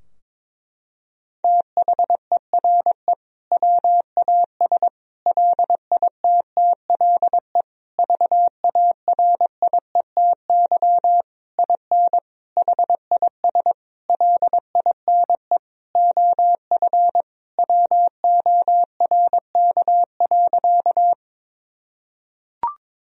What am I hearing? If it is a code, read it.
Morse code, heard as a keyed tone, about 22 wpm, 700 Hz. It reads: THERE WAS LITTLE VARIETY IN HIS LINE OF WORK.
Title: Sentences from Top 1000 Words - Review Part 4 - 22wpm